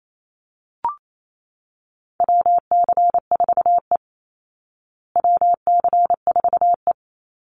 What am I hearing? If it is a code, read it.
Morse code, heard as a keyed tone, about 28 wpm, 700 Hz. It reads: WC4E WC4E
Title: Call Signs Repeated 2x - 28wpm